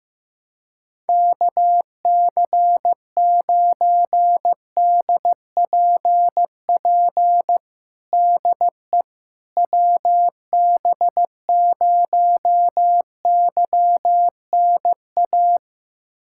Morse code, heard as a keyed tone, about 15 wpm, 700 Hz. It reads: KC9DPP DE WB0YNA
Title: USA Callsign DE Callsign - 6 Characters - 15wpm